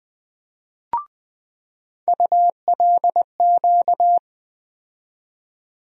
Morse code, heard as a keyed tone, about 21 wpm, 700 Hz. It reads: ULQ
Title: USA Call Sign Postfixes - 3 Letters 20wpm